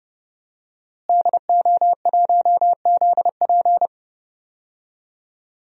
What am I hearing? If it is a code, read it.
Morse code, heard as a keyed tone, about 30 wpm, 700 Hz. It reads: DO1ZP
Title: DXCC-100 Call Signs - 5 Characters - Repeated 3x - 30wpm